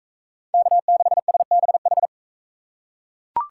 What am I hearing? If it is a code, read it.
Morse code, heard as a keyed tone, about 40 wpm, 700 Hz. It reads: K6SBH